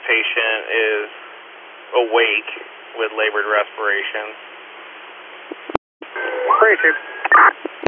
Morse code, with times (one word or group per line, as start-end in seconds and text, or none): none